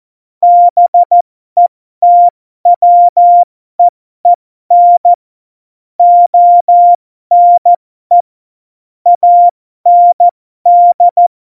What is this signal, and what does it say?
Morse code, heard as a keyed tone, about 14 wpm, spaced as for 12 wpm, 700 Hz. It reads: BETWEEN ONE AND
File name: Morsekurs / Witze